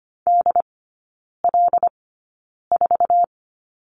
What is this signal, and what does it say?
Morse code, heard as a keyed tone, about 25 wpm, spaced as for 9 wpm, 700 Hz. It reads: DL4
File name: course / 25:10wpm / 100 Lesson 17e - Letter u - Three Characters 25wpm